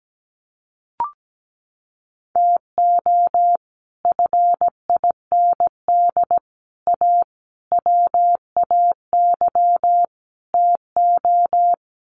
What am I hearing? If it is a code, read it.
Morse code, heard as a keyed tone, about 17 wpm, 700 Hz. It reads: TO FIND A WAY TO
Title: Sets of 5 Words 17wpm